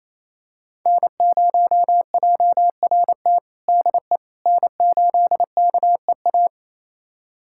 Morse code, heard as a keyed tone, about 28 wpm, 700 Hz. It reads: N0JRT DE N8KEA
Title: USA Callsign DE Callsign - 5 Characters - 28wpm